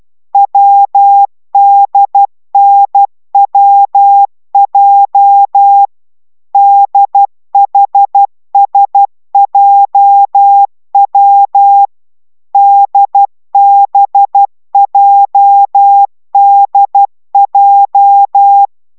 Morse code, heard as a keyed tone, about 12 wpm, 800 Hz. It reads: WDNWJ DHSJW DBJDJ